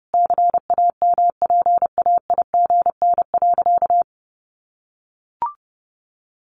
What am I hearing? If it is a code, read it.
Morse code, heard as a keyed tone, about 30 wpm, 700 Hz. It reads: CAMPAIGN.
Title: Sentences from Top 1200 Words 30wpm